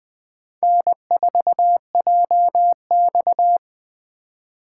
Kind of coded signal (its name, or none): Morse code